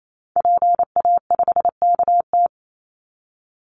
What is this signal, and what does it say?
Morse code, heard as a keyed tone, about 28 wpm, 700 Hz. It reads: PA5KT
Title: Call Signs Repeated 3x - 28wpm